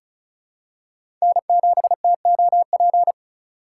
Morse code, heard as a keyed tone, about 34 wpm, 700 Hz. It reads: N7TOP
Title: USA Call Signs - 5 Characters - 3x Character Spacing - 35wpm